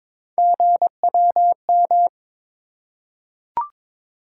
Morse code, heard as a keyed tone, about 22 wpm, 700 Hz. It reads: GWM
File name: practice / 22wpm / Three Letter-Number Combinations - 22wpm